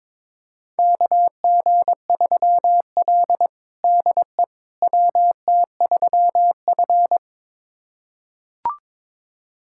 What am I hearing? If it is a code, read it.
Morse code, heard as a keyed tone, about 21 wpm, 700 Hz. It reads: KG3L DE WT3F
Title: USA Callsign DE Callsign - 4 Characters - 22wpm